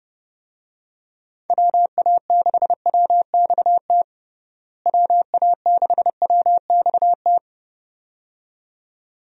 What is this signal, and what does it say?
Morse code, heard as a keyed tone, about 30 wpm, 700 Hz. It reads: WA6WXT WA6WXT
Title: USA Call Signs - 6 Characters - Repeated 2x - 30wpm